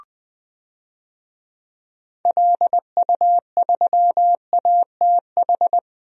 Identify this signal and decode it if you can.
Morse code, heard as a keyed tone, about 20 wpm, 700 Hz. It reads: LU3ATH
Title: DXCC-100 Call Signs - 6 Characters - 20wpm